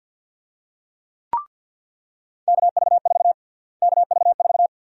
Morse code, heard as a keyed tone, about 48 wpm, 700 Hz. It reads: KV4 KV4
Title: USA Call Sign Prefixes - Repeated 2x - 50wpm